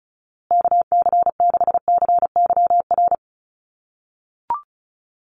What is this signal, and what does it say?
Morse code, heard as a keyed tone, about 34 wpm, 700 Hz. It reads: KC6CYR
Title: USA Call Signs Repeated 2x 35wpm